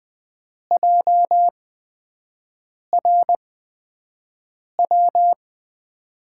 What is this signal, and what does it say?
Morse code, heard as a keyed tone, about 20 wpm, spaced as for 5 wpm, 700 Hz. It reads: JRW
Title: USA Call Signs - 6 Characters - 8x Character Spacing - 20wpm